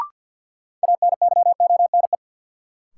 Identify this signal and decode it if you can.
Morse code, heard as a keyed tone, about 53 wpm, 700 Hz. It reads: ANYONE
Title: Sentences from Top 900 Words - New Word Drill - 50wpm